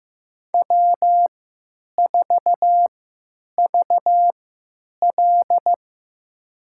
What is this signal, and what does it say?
Morse code, heard as a keyed tone, about 15 wpm, spaced as for 9 wpm, 700 Hz. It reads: W4VL
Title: USA Call Signs - 4 Characters - 3x Character Spacing - 15wpm